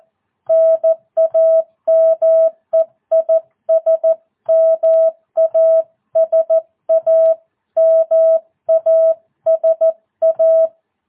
Morse code, heard as a keyed tone, about 14 wpm, 650 Hz. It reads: NAME IS MASA MASA